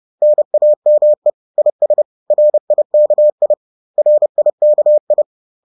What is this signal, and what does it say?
Morse code, heard as a keyed tone, about 30 wpm, 600 Hz. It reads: NAME IS RIKI RIKI
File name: 30wpm-4